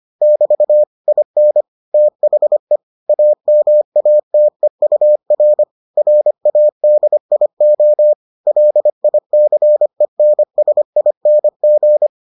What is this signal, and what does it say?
Morse code, heard as a keyed tone, about 25 wpm, 600 Hz. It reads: <BT> IN THE AMATEUR RADIO LICENSING